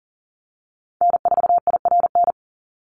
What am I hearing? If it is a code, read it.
Morse code, heard as a keyed tone, about 40 wpm, 700 Hz. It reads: N4IRN
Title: Call Signs Repeated 3x - 40wpm